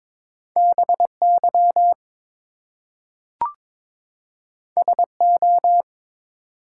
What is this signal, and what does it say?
Morse code, heard as a keyed tone, about 22 wpm, 700 Hz. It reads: BY SO
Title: Two Letter Words - 22wpm